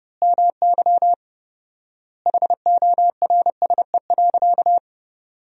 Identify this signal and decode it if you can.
Morse code, heard as a keyed tone, about 30 wpm, 700 Hz. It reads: MY HORSE.
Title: 4x Word-Spacing - Sentences from Top 500 Words - 30wpm